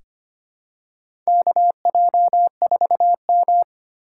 Morse code, heard as a keyed tone, about 25 wpm, 700 Hz. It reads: KJ4M
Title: USA Call Signs - 2x Character Spacing - 25wpm